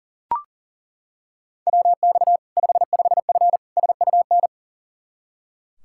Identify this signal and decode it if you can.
Morse code, heard as a keyed tone, about 40 wpm, 700 Hz. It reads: WX 55F SUN
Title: Practice QSO - Exchange 2 - Info - 40wpm